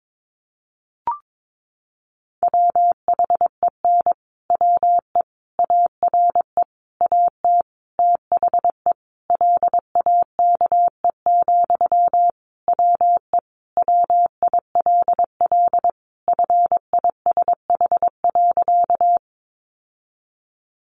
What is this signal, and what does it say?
Morse code, heard as a keyed tone, about 22 wpm, 700 Hz. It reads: WHEN WE ARE AT THE LAKE, WE WILL FISH.
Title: Sentences from Top 1100 Words 22wpm